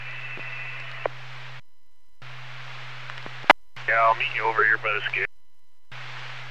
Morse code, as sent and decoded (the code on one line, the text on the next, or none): none